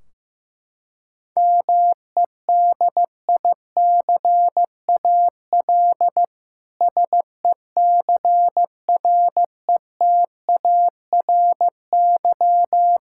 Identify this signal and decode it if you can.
Morse code, heard as a keyed tone, about 15 wpm, 700 Hz. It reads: MEDICAL SECRETARY
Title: Practice QSO - Exchange 3 - More Info - 15wpm